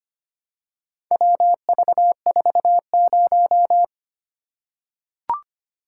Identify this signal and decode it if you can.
Morse code, heard as a keyed tone, about 25 wpm, 700 Hz. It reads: WV40
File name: practice / 25wpm / Four Letter-Number Combinations - 25wpm